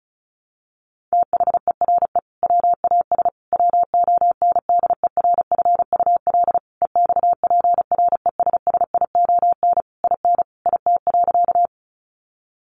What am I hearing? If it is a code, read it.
Morse code, heard as a keyed tone, about 34 wpm, 700 Hz. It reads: THERE WAS WONDERFUL EXPRESSION IN IT.
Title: Sentences from Top 1200 Words 35wpm